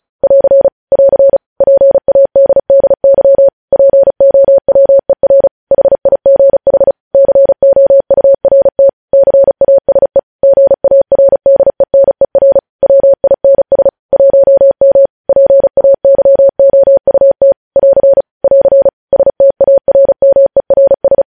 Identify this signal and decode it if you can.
Morse code, heard as a keyed tone, about 36 wpm, 550 Hz. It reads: <AR> <AR> PADDY POWER HIGH COURT CASE GARDENER WINS 1M PAYOUT <AR> <AR> STARMERS
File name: podcast / News headlines-1x-35x35